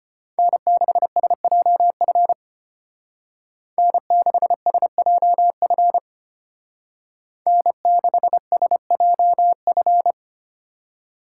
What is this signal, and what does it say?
Morse code, heard as a keyed tone, about 32 wpm, 700 Hz. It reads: N6SJF N6SJF N6SJF